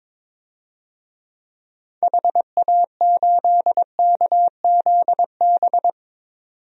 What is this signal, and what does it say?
Morse code, heard as a keyed tone, about 22 wpm, 700 Hz. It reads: HA8KZB
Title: DXCC-100 Call Signs - 6 Characters - 22wpm